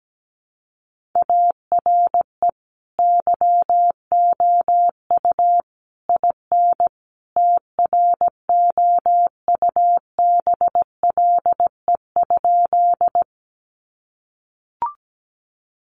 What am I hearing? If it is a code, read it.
Morse code, heard as a keyed tone, about 17 wpm, 700 Hz. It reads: ARE YOU IN TROUBLE?
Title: Sentences from Top 1200 Words 17wpm